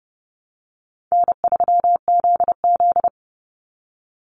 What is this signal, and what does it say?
Morse code, heard as a keyed tone, about 30 wpm, 700 Hz. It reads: N3ZZ
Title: Easy Call Signs Repeated 3x - 30wpm